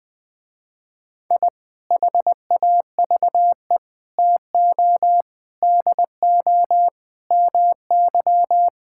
Morse code, heard as a keyed tone, about 21 wpm, 700 Hz. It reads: I HAVE TO DO MY
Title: Sentences with 3-Word Phrases - Spoken Beginning - 20wpm